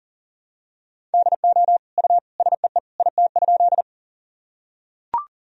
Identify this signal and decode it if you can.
Morse code, heard as a keyed tone, about 40 wpm, 700 Hz. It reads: DO U SEE IT?
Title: Sentences from Top 100 Words and Abbreviations 40wpm